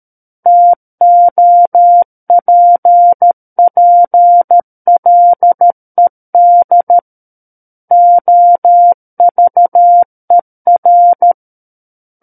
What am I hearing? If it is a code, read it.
Morse code, heard as a keyed tone, about 13 wpm, 700 Hz. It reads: TOPPLED OVER
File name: Book One / 04-Chapter 4 - 13 WPM